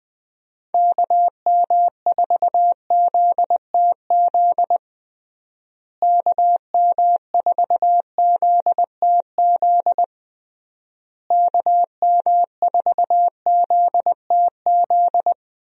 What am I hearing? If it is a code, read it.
Morse code, heard as a keyed tone, about 21 wpm, 700 Hz. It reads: KM4ZTZ KM4ZTZ KM4ZTZ
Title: USA Call Signs - 6 Characters - Repeated 3x - 20wpm